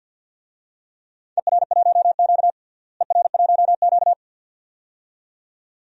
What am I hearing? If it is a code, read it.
Morse code, heard as a keyed tone, about 48 wpm, 700 Hz. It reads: ER1Q ER1Q
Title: DXCC-100 Call Signs - 4 Characters - Repeated 2x - 50wpm